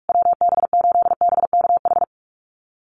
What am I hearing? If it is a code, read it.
Morse code, heard as a keyed tone, about 44 wpm, 700 Hz. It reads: WB8BKH
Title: USA Call Signs Repeated 3x 45wpm